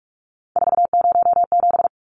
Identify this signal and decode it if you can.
Morse code, heard as a keyed tone, about 44 wpm, 700 Hz. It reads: 407
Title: Numbers - 3-Digit 45wpm